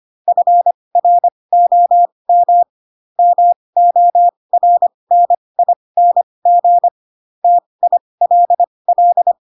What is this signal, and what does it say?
Morse code, heard as a keyed tone, about 25 wpm, spaced as for 20 wpm, 700 Hz. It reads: FROM MORNING TILL
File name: Pinocchio / Pinocchio ch4 15 25 ik1hge